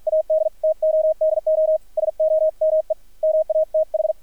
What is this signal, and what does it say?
Morse code, heard as a keyed tone, about 50 wpm, spaced as for 44 wpm, 630 Hz, with radio noise. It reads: AD TO DO SOME MATH
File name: archived / CRSnet